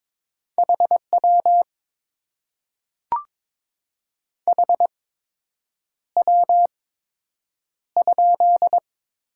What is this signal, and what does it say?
Morse code, heard as a keyed tone, about 23 wpm, spaced as for 8 wpm, 700 Hz. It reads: HW HW?
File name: Top 90 QSO Elements - 8x Character Spacing - 22wpm